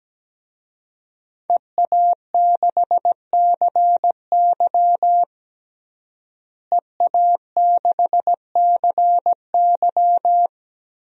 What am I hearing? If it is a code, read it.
Morse code, heard as a keyed tone, about 17 wpm, 700 Hz. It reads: EA6CY EA6CY
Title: DXCC-100 Call Signs - 5 Characters - Repeated 2x - 17wpm